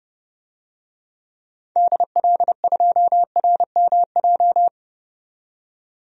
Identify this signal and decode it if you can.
Morse code, heard as a keyed tone, about 30 wpm, 700 Hz. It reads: DL2RMJ
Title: DXCC-100 Call Signs - 6 Characters - 30wpm